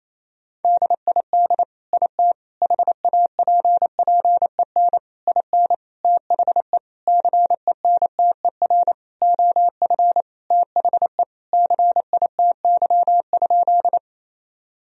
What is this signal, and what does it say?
Morse code, heard as a keyed tone, about 28 wpm, 700 Hz. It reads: DID IT HAPPEN IN THE CENTER OF THE CITY?
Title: Sentences from Top 1000 Words - Review Part 2 - 28wpm